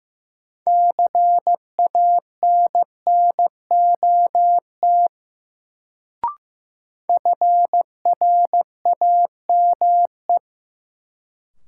Morse code, heard as a keyed tone, about 15 wpm, 700 Hz. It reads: CANNOT FRAME
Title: Sentences from Top 1000 Words - Review Part 6 - New Word Drill - 15wpm